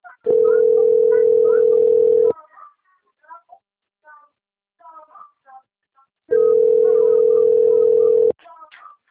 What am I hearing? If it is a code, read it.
Ringback tone (North America), 440 + 480 Hz.